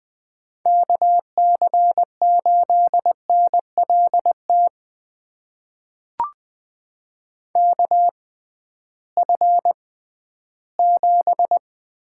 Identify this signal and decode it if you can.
Morse code, heard as a keyed tone, about 20 wpm, spaced as for 13 wpm, 700 Hz. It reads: KC8NLT KF7
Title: USA Call Signs - 6x Character Spacing - 20wpm